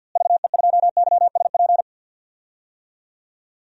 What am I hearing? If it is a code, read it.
Morse code, heard as a keyed tone, about 52 wpm, 700 Hz. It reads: VE2YSP
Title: DXCC-100 Call Signs - 6 Characters - Repeated 2x - 50wpm